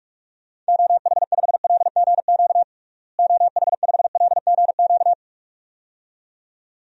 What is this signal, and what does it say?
Morse code, heard as a keyed tone, about 46 wpm, 700 Hz. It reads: OH5LGQ OH5LGQ
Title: DXCC-100 Call Signs - 6 Characters - Repeated 2x - 45wpm